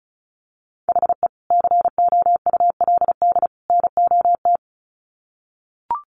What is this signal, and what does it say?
Morse code, heard as a keyed tone, about 34 wpm, 700 Hz. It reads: HE COULD NOT
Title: Sets of 3 Words 35wpm